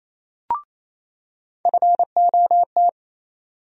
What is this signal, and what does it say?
Morse code, heard as a keyed tone, about 28 wpm, 700 Hz. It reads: FOT
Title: USA Call Sign Postfixes - 3 Letters 28wpm